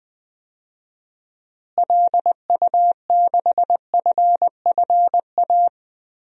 Morse code, heard as a keyed tone, about 20 wpm, 700 Hz. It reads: LU6FFA
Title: DXCC-100 Call Signs - 6 Characters - Repeated 3x - 20wpm